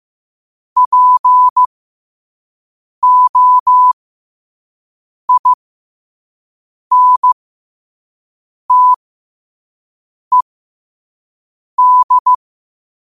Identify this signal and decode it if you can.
Morse code, heard as a keyed tone, about 15 wpm, spaced as for 5 wpm, 1000 Hz. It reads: POINTED
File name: qotdfiles / 5wpm